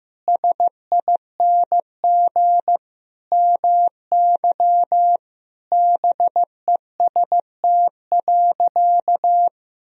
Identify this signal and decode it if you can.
Morse code, heard as a keyed tone, about 15 wpm, 700 Hz. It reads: SING MY BEST.
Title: Sentences from Top 1000 Words - Review Part 2 - 15wpm